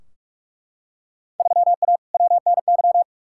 Morse code, heard as a keyed tone, about 42 wpm, 700 Hz. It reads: 3A WNY
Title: Field Day Exchanges - 45wpm